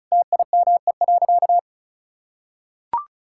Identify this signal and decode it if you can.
Morse code, heard as a keyed tone, about 36 wpm, 700 Hz. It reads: TIME.
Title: Sentences from Top 1000 Words - Review Part 5 - 35wpm